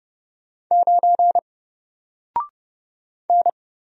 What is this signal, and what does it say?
Morse code, heard as a keyed tone, about 30 wpm, 700 Hz. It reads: 9 N
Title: Single Letter-Number Alt 30wpm